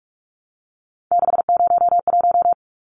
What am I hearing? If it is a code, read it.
Morse code, heard as a keyed tone, about 44 wpm, 700 Hz. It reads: 601
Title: Numbers - 3-Digit 45wpm